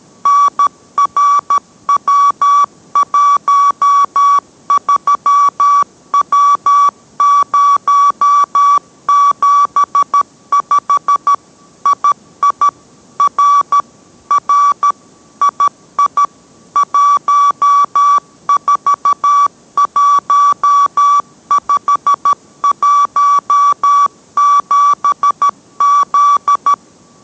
Morse code, heard as a keyed tone, about 13 wpm, 1140 Hz. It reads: NRW13W075 II R R II 141517Z